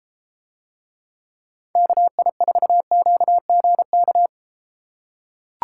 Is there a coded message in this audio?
Yes: Morse code from 1.8 to 4.3 s, heard as a keyed tone, about 31 wpm, 700 Hz. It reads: KI4QGK